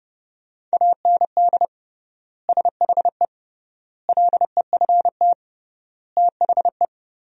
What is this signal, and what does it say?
Morse code, heard as a keyed tone, about 29 wpm, 700 Hz. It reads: AND SHE LEFT THE
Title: 3x Word-Spacing - Sentences from Top 300 Words - 30wpm